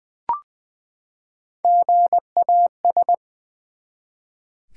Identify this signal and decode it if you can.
Morse code, heard as a keyed tone, about 20 wpm, 700 Hz. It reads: GAS
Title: Sentences from Top 1100 Words - New Word Drill v2 - 20wpm